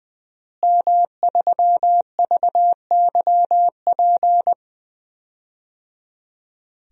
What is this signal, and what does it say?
Morse code, heard as a keyed tone, about 21 wpm, 700 Hz. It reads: M3VYP
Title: DXCC-100 Call Signs - 5 Characters - Repeated 3x - 20wpm